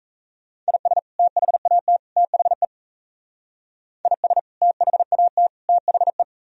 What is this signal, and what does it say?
Morse code, heard as a keyed tone, about 33 wpm, 700 Hz. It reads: IS THAT THE IS THAT THE
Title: Speed-Racing - Sets of 3 Words 28wpm